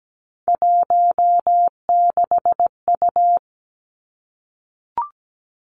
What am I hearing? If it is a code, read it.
Morse code, heard as a keyed tone, about 17 wpm, 700 Hz. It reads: 16U
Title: Three Challenging Letter-Number Combinations 17wpm